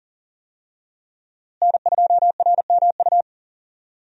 Morse code, heard as a keyed tone, about 40 wpm, 700 Hz. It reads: N2RMU
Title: USA Call Signs - 5 Characters - Repeated 2x - 40wpm